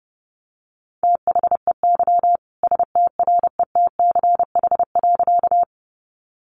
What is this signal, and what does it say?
Morse code, heard as a keyed tone, about 30 wpm, 700 Hz. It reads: THEY STRETCH.
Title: Easy Phrases 30wpm